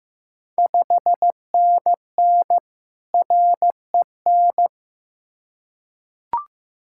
Morse code, heard as a keyed tone, about 15 wpm, 700 Hz. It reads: 5NN REN
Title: Ontario QSO Party Exchange 15wpm